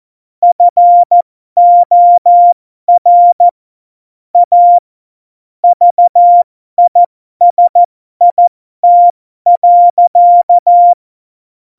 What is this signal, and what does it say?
Morse code, heard as a keyed tone, about 14 wpm, spaced as for 12 wpm, 700 Hz. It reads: FOR A VISIT.
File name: Morsekurs / Witze